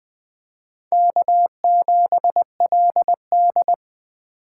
Morse code, heard as a keyed tone, about 20 wpm, 700 Hz. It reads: K7LD